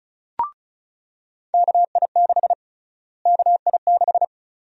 Morse code, heard as a keyed tone, about 34 wpm, 700 Hz. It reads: KI6 KI6
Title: USA Call Sign Prefixes - Repeated 2x - 35wpm